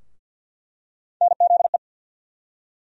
Morse code, heard as a keyed tone, about 47 wpm, 700 Hz. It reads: NZE
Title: USA Call Sign Postfixes - 3 Letters 50wpm